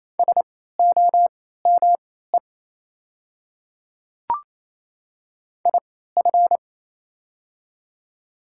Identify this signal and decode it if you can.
Morse code, heard as a keyed tone, about 27 wpm, spaced as for 16 wpm, 700 Hz. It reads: SOME IF
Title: 3x Character Spacing - Top 100 Words - 28wpm